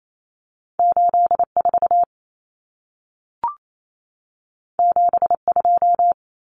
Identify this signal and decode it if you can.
Morse code, heard as a keyed tone, about 28 wpm, 700 Hz. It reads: 84 72
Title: Common Cut Numbers - 2-Digit 28wpm